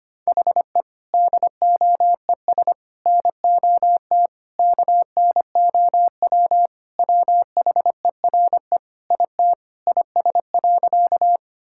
Morse code, heard as a keyed tone, about 25 wpm, 700 Hz. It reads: HE DOES NOT KNOW WHERE IT IS.
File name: practice / 25wpm / Sentences with 4-Word Phrases - 25wpm